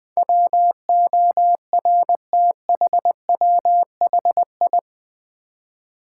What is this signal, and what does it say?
Morse code, heard as a keyed tone, about 20 wpm, 700 Hz. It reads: WORTHWHI
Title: Build A Word - 4 Letters At A Time - 10-Letter Words - 20wpm